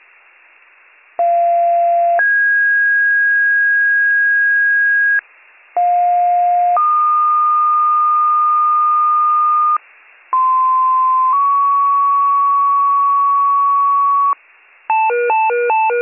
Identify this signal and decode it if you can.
Two-tone page, heard as a one-second tone, then a three-second tone, about 690 then 1650 Hz, about 710 then 1220 Hz, about 1040 then 1120 Hz. It is heard from 1.2 to 14.3 s.